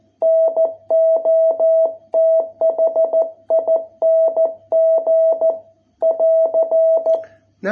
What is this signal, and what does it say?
Morse code, heard as a keyed tone, about 14 wpm, 660 Hz. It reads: NOTHING <AR>